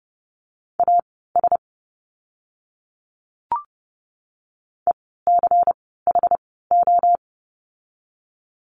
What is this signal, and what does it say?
Morse code, heard as a keyed tone, about 30 wpm, spaced as for 17 wpm, 700 Hz. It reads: AS ECHO
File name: Lesson 16f - Letter c - Words 30wpm